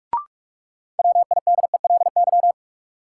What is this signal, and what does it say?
Morse code, heard as a keyed tone, about 41 wpm, 700 Hz. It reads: WIDELY
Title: Sentences from Top 1200 Words - New Word Drill - 45wpm